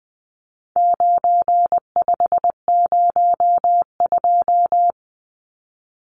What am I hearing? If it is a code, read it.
Morse code, heard as a keyed tone, about 20 wpm, 700 Hz. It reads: 9502
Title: Numbers - 4-Digit 20wpm